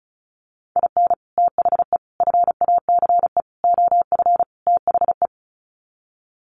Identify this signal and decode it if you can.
Morse code, heard as a keyed tone, about 34 wpm, 700 Hz. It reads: IN THE FACE OF THE
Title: Sets of 5 Words 35wpm